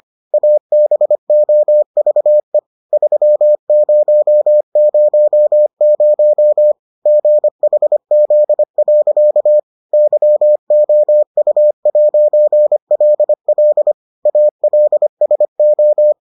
Morse code, heard as a keyed tone, about 25 wpm, 600 Hz. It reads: ABOVE 3000 GHZ. YOU'LL ALSO